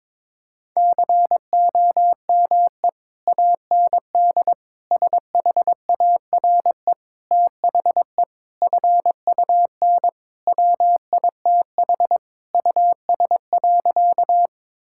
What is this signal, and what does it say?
Morse code, heard as a keyed tone, about 22 wpm, 700 Hz. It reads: COME AND SHARE THE FUN WITH US.
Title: Sentences from Top 1000 Words - Review Part 6 - 22wpm